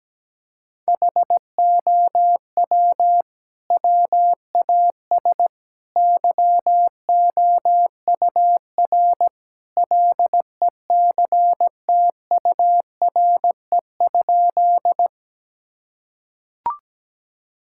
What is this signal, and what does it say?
Morse code, heard as a keyed tone, about 17 wpm, 700 Hz. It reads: HOW WAS YOUR LECTURE?
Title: Sentences from Top 1600 Words 17wpm